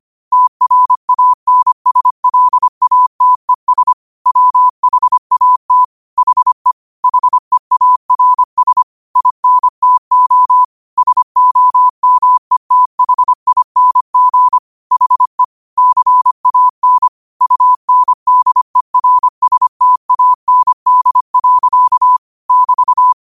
Morse code, heard as a keyed tone, about 25 wpm, 1000 Hz. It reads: TRANSLATES WHAT HE HEARS INTO SOMETHING HE CAN UNDERSTAND. <BT>
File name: 25wpm